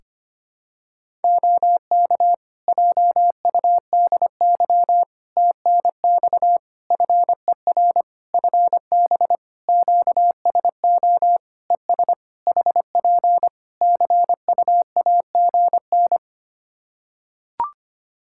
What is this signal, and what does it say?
Morse code, heard as a keyed tone, about 25 wpm, 700 Hz. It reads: OK JUDY TNX FER FB QSO ES HP CUAGN
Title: Practice QSO - Exchange 4 - Ending - 25wpm